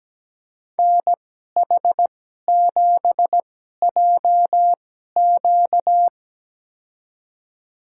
Morse code, heard as a keyed tone, about 17 wpm, spaced as for 12 wpm, 700 Hz. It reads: NH7JQ
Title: USA Call Signs - 5 Characters - 2x Character Spacing - 17wpm